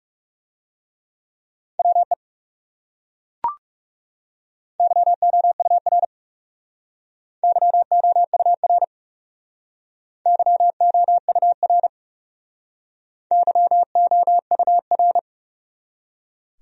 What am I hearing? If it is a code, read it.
Morse code, heard as a keyed tone, about 40 wpm, 700 Hz. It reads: WE YOUR YOUR YOUR YOUR